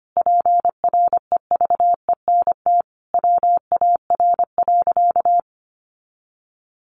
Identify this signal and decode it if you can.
Morse code, heard as a keyed tone, about 25 wpm, 700 Hz. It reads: PREVENT WAR.
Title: Sentences from Top 1100 Words 25wpm